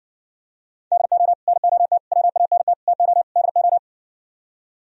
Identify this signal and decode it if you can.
Morse code, heard as a keyed tone, about 59 wpm, 700 Hz. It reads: DO NOT WANT TO DO